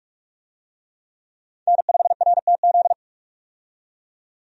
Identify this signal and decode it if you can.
Morse code, heard as a keyed tone, about 46 wpm, 700 Hz. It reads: N5RTZ
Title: USA Call Signs - 5 Characters - 45wpm